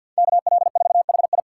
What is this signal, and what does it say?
Morse code, heard as a keyed tone, about 47 wpm, 690 Hz. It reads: KL4HI
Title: USA Call Signs - 5 Characters - 6x Character Spacing - 50wpm